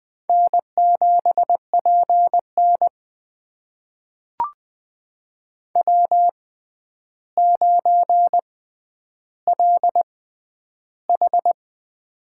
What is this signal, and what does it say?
Morse code, heard as a keyed tone, about 20 wpm, spaced as for 10 wpm, 700 Hz. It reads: N7PN W9LH
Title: USA Call Signs - 4 Characters - 6x Character Spacing - 20wpm